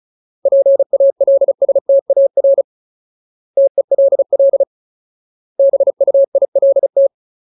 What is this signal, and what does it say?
Morse code, heard as a keyed tone, about 34 wpm, 550 Hz. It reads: PALSTAR TELL BUILT
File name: Ultimate Word List / 35WPM Ultimate